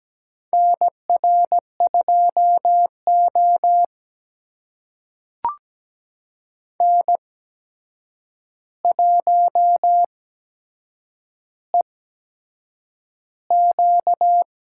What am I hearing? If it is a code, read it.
Morse code, heard as a keyed tone, about 17 wpm, spaced as for 7 wpm, 700 Hz. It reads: NR2O N1EQ
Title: USA Call Signs - 4 Characters - 8x Character Spacing - 17wpm